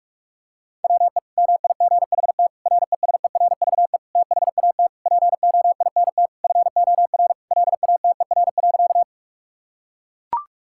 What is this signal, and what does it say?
Morse code, heard as a keyed tone, about 41 wpm, 700 Hz. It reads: WE MIGHT RESERVE THAT POINT FOR LATER.
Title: Sentences from Top 1000 Words - Review Part 6 - 45wpm